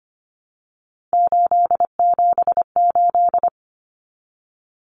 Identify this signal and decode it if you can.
Morse code, heard as a keyed tone, about 25 wpm, 700 Hz. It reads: 878
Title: Common Cut Numbers - 3-Digit 25wpm